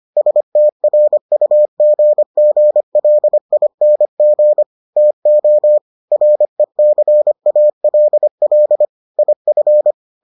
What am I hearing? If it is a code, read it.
Morse code, heard as a keyed tone, about 25 wpm, 600 Hz. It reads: STRUGGLING TO RECALL IF